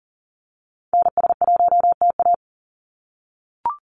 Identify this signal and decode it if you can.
Morse code, heard as a keyed tone, about 40 wpm, 700 Hz. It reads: NS1TA
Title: NA SOTA Activator Callsigns 40wpm